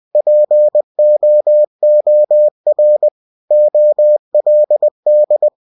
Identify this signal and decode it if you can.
Morse code, heard as a keyed tone, about 20 wpm, 600 Hz. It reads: POOR OLD
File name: cw-books / winnie-the-pooh / winnie-20wpm-page-0088